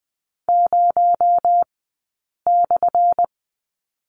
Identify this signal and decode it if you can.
Morse code, heard as a keyed tone, about 20 wpm, spaced as for 8 wpm, 700 Hz. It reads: 0/
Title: 217 Lesson 35d - Number 0 - Two Characters 20wpm